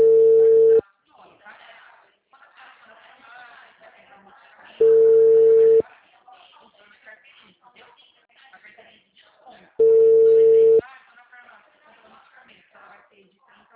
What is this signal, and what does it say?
Ringback tone (China, 450 Hz), with speech.